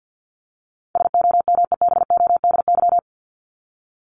Morse code, heard as a keyed tone, about 50 wpm, 700 Hz. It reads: SOMEBODY
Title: Top 1751-2000 Words 50wpm